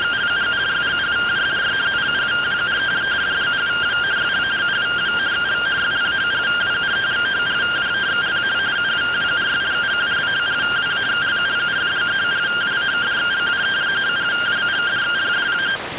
Two-tone teleprinter signal, about 170 Hz shift.